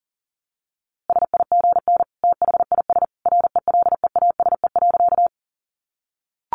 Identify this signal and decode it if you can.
Morse code, heard as a keyed tone, about 40 wpm, 700 Hz. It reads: SIGN THIS RELEASE.
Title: A Sound Of Thunder 40wpm